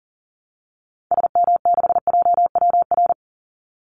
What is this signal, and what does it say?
Morse code, heard as a keyed tone, about 40 wpm, 700 Hz. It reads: SM6JWR
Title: Call Signs Repeated 3x - 40wpm